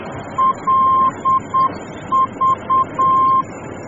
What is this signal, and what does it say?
Morse code, heard as a keyed tone, about 8 wpm, 1050 Hz, with noise. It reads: LV
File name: germany / mx / LV-D